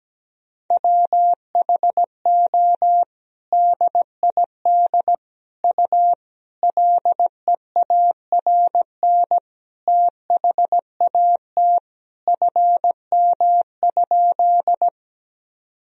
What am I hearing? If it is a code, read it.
Morse code, heard as a keyed tone, about 17 wpm, 700 Hz. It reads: WHO DID U LEARN THAT FM?
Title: Sentences from Top 300 Words and Abbreviations 17wpm